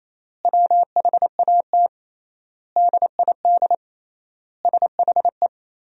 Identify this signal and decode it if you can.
Morse code, heard as a keyed tone, about 28 wpm, 700 Hz. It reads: WHAT DID SHE
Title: 3x Word-Spacing - Sentences from Top 500 Words - 28wpm